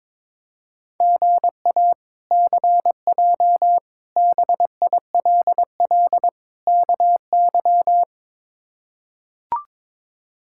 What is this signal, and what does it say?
Morse code, heard as a keyed tone, about 22 wpm, 700 Hz. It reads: GA CJ BILL KY
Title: K1USN SST QSO Practice - 22wpm